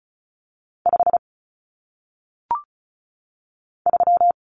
Morse code, heard as a keyed tone, about 35 wpm, 700 Hz. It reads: 5 3
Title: Common Cut Numbers - 1-Digit 35wpm